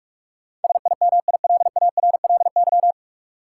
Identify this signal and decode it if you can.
Morse code, heard as a keyed tone, about 46 wpm, 700 Hz. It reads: SIMILARLY
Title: Sentences from Top 1400 Words - New Word Drill Alt - 45wpm